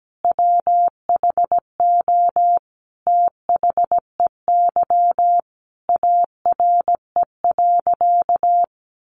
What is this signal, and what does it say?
Morse code, heard as a keyed tone, about 17 wpm, 700 Hz. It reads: WHO THEY ARE.
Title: Sentences from Top 700 Words 17wpm